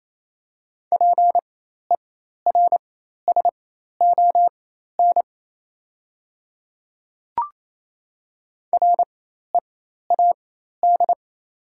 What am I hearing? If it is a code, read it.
Morse code, heard as a keyed tone, about 28 wpm, spaced as for 13 wpm, 690 Hz. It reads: PERSON READ
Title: 4x Character Spacing - Top 500 Words - 28wpm